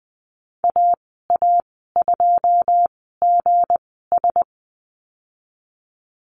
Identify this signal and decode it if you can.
Morse code, heard as a keyed tone, about 19 wpm, spaced as for 14 wpm, 700 Hz. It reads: AA2GS
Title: 167 Lesson 27g - Number 7 - Call Signs 20wpm